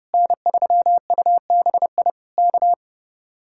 Morse code, heard as a keyed tone, about 30 wpm, 700 Hz. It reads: N3UBI K